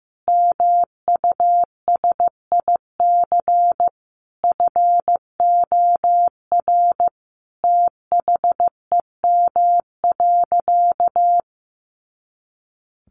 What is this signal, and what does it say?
Morse code, heard as a keyed tone, about 14 wpm, 700 Hz. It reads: MUSIC FOR THEM.
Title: Sentences from Top 300 Words 15wpm